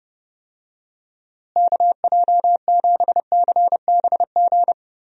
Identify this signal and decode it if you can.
Morse code, heard as a keyed tone, about 30 wpm, 700 Hz. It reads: KJ7CBG
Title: USA Call Signs - 6 Characters - Repeated 2x - 30wpm